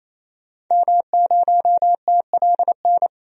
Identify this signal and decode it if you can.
Morse code, heard as a keyed tone, about 27 wpm, 700 Hz. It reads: M0TLN
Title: DXCC-100 Call Signs - 5 Characters - 28wpm